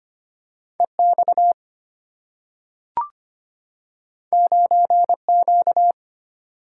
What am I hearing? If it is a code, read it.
Morse code, heard as a keyed tone, about 25 wpm, 700 Hz. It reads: EX 9Q
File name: Two Letter-Number Combinations - 25wpm